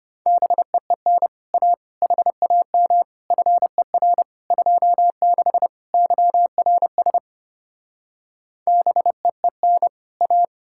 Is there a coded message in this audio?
Yes: Morse code from 0.3 to 10.4 s, heard as a keyed tone, about 33 wpm, 700 Hz. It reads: BEEN A HAM FER 26 YRS BEEN A